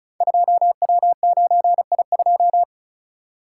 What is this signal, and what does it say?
Morse code, heard as a keyed tone, about 36 wpm, 700 Hz. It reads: 2W9I2
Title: Code Groups - 5 Letter:Numbers - 35wpm